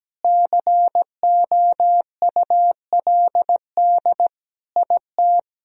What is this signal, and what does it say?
Morse code, heard as a keyed tone, about 17 wpm, 700 Hz. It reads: COULD IT
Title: Sentences from Top 1000 Words - Review Part 2 - 17wpm